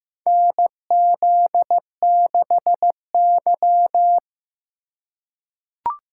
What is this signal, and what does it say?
Morse code, heard as a keyed tone, about 15 wpm, 700 Hz. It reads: NZ6Y
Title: USA Call Signs - 4 Characters - 8x Character Spacing - 15wpm